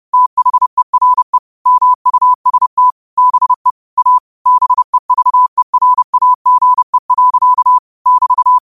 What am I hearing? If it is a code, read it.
Morse code, heard as a keyed tone, about 30 wpm, 1000 Hz. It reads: THERE MUST BE A BEVERAGE. <BT>
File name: qotdfiles / 30wpm